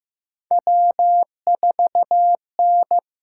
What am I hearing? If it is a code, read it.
Morse code, heard as a keyed tone, about 15 wpm, 700 Hz. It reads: W4N
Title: USA Call Signs - 8x Character Spacing - 15wpm